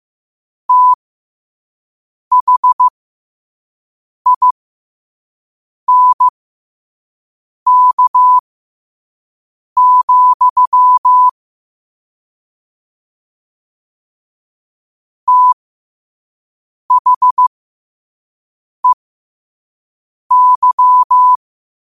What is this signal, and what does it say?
Morse code, heard as a keyed tone, about 15 wpm, spaced as for 5 wpm, 1000 Hz. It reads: THINK, THEY
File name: qotdfiles / 5wpm